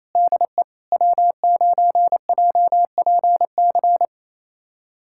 Morse code, heard as a keyed tone, about 28 wpm, 700 Hz. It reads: DE W9JPC
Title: USA Callsign DE Callsign - 5 Characters - 28wpm